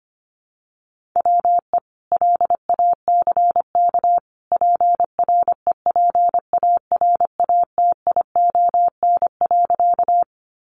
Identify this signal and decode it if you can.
Morse code, heard as a keyed tone, about 25 wpm, 700 Hz. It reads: WE LACK PREPARATION.
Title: Sentences from Top 1400 Words 25wpm